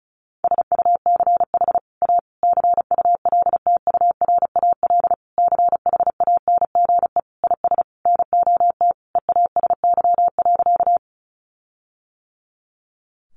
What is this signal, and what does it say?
Morse code, heard as a keyed tone, about 34 wpm, 700 Hz. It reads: SUCH A CULTURAL CHANGE IS NOT EASY.
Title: Sentences from Top 1500 Words 35wpm